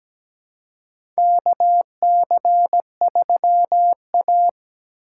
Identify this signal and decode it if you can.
Morse code, heard as a keyed tone, about 17 wpm, 700 Hz. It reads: KC3A
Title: USA Call Signs - 4 Characters - 2x Character Spacing - 17wpm